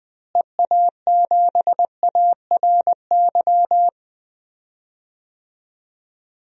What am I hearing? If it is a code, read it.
Morse code, heard as a keyed tone, about 19 wpm, 700 Hz. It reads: EA7ARY
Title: DXCC-100 Call Signs - 6 Characters - Repeated 3x - 20wpm